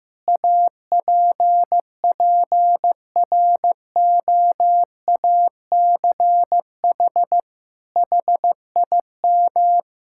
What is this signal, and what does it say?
Morse code, heard as a keyed tone, about 16 wpm, 700 Hz. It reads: APPROACH HIM
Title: Sentences from Top 1000 Words - Review Part 6 - 15wpm